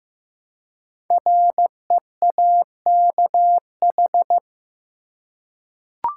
Morse code, heard as a keyed tone, about 15 wpm, 700 Hz. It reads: REAKH
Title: Code Groups - 5 Letter:Numbers - 15wpm